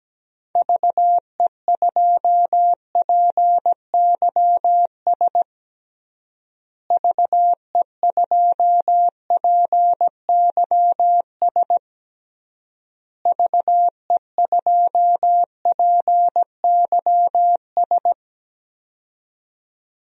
Morse code, heard as a keyed tone, about 17 wpm, 700 Hz. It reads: VE2PYS VE2PYS VE2PYS
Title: DXCC-100 Call Signs - 6 Characters - Repeated 3x - 17wpm